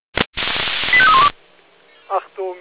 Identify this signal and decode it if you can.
Five-tone selective call (ZVEI), heard as quick runs of five tones, tones of about 70 ms, with speech. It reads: 85312